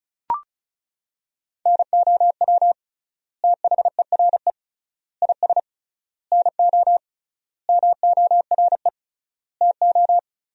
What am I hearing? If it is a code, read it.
Morse code, heard as a keyed tone, about 34 wpm, 700 Hz. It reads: NOW THERE IS NO MORE TO
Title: 3x Word-Spacing - Sentences from Top 100 Words - 35wpm